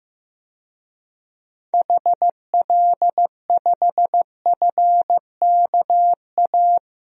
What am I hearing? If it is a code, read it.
Morse code, heard as a keyed tone, about 15 wpm, 700 Hz. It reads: HL5FKA
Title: DXCC-100 Call Signs - 6 Characters - 15wpm